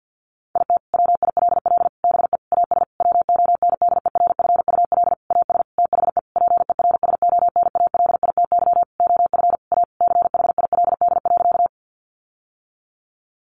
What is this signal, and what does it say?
Morse code, heard as a keyed tone, about 50 wpm, 700 Hz. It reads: IT WILL BE AS WONDERFUL AS THE PERSONALITY OF A CHILD.
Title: Sentences from Top 1400 Words 50wpm